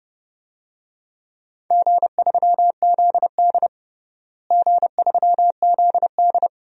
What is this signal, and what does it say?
Morse code, heard as a keyed tone, about 31 wpm, 700 Hz. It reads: G3ZD G3ZD
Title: DXCC-100 Call Signs - 4 Characters - Repeated 3x - 30wpm